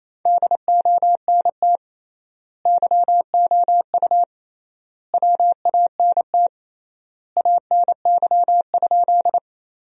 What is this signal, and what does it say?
Morse code, heard as a keyed tone, about 27 wpm, 700 Hz. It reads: DONT YOU WANT ANY?
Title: 3x Word-Spacing - Sentences from Top 300 Words - 28wpm